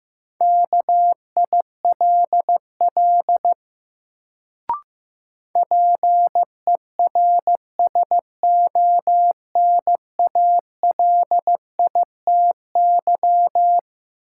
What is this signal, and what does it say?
Morse code, heard as a keyed tone, about 15 wpm, 700 Hz. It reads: KILL PERSONALITY
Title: Sentences from Top 1400 Words - New Word Drill - 15wpm